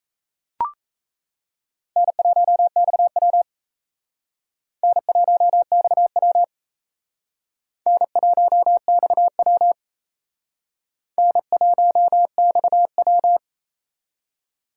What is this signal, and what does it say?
Morse code, heard as a keyed tone, about 34 wpm, 700 Hz. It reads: N1XW N1XW N1XW N1XW
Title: Speed Racing - USA Call Signs 28wpm